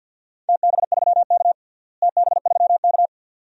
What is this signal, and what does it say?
Morse code, heard as a keyed tone, about 51 wpm, 700 Hz. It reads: TB3K TB3K
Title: DXCC-100 Call Signs - 4 Characters - Repeated 2x - 50wpm